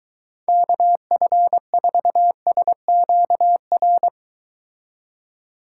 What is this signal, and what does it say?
Morse code, heard as a keyed tone, about 22 wpm, 700 Hz. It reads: KF4SQR